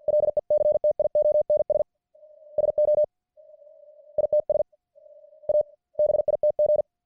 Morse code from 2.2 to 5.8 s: SO ITS A